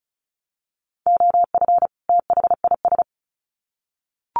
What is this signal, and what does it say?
Morse code, heard as a keyed tone, about 34 wpm, 700 Hz. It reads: OF THIS
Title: Sets of 2 Words 35wpm